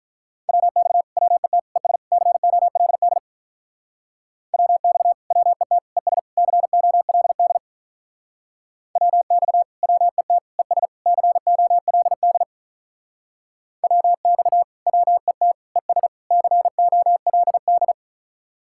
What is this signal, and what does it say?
Morse code, heard as a keyed tone, about 45 wpm, 700 Hz. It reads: WX WET ES COLD WX WET ES COLD WX WET ES COLD WX WET ES COLD